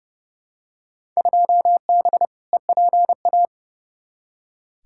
Morse code, heard as a keyed tone, about 31 wpm, 700 Hz. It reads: 2B EPA
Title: Field Day Exchanges Encore - 30wpm